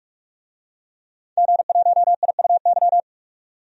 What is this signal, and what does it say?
Morse code, heard as a keyed tone, about 46 wpm, 700 Hz. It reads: G1IUY
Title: DXCC-100 Call Signs - 5 Characters - 45wpm